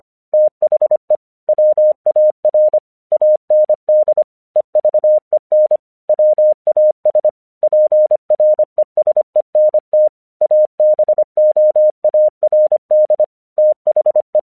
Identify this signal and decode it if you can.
Morse code, heard as a keyed tone, about 25 wpm, 610 Hz. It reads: THE WAR AND EVEN WAS PRESENT ABOARD THE
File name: 1 English 25wpm / AQC09